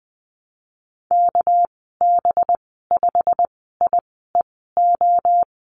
Morse code, heard as a keyed tone, about 19 wpm, spaced as for 14 wpm, 700 Hz. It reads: KB5IEO
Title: 200 Lesson 32g - Letter k - Call Signs 20wpm